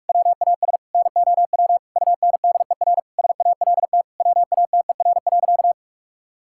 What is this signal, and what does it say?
Morse code, heard as a keyed tone, about 43 wpm, 700 Hz. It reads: WAS NOW UNDER SALT WATER.
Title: Sentences from Top 1000 Words - Review Part 5 - 45wpm